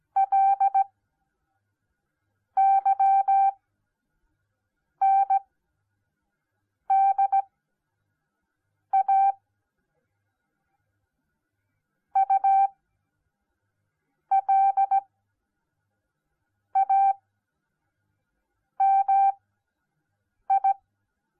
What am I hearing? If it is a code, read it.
Morse code, heard as a keyed tone, about 17 wpm, spaced as for 5 wpm, 780 Hz. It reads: LYNDA ULAMI